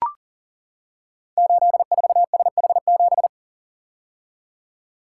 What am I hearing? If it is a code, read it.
Morse code, heard as a keyed tone, about 42 wpm, 690 Hz. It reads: 84SH7
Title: Code Groups - 5 Letter:Numbers - 40wpm